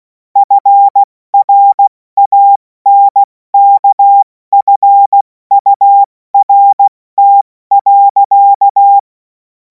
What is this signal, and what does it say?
Morse code, heard as a keyed tone, about 16 wpm, 800 Hz. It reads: FRANKFURT.
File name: Morsekurs / Witze / Witz 11. August 2026 70 F80